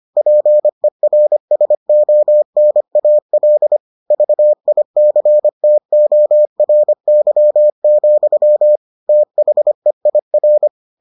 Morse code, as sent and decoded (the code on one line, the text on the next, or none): .--. .-. ... --- -. .- .-.. / ...- .. -.-. - --- .-. -.-- --..-- / - .... .. .-.
PERSONAL VICTORY, THEIR